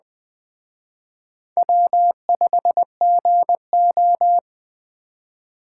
Morse code, heard as a keyed tone, about 20 wpm, 700 Hz. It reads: W5GO
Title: Speed Racing - USA Call Signs 17wpm